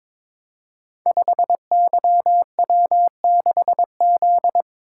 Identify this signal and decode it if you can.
Morse code, heard as a keyed tone, about 22 wpm, 700 Hz. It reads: 5YW6Z
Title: Code Groups - 5 Letter:Numbers - 22wpm